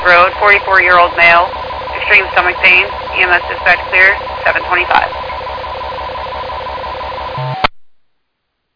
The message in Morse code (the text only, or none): none